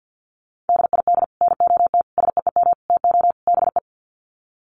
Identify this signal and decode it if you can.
Morse code, heard as a keyed tone, about 50 wpm, 700 Hz. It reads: DID NOT SEEM TO BE
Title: Sets of 5 Words Encore 50wpm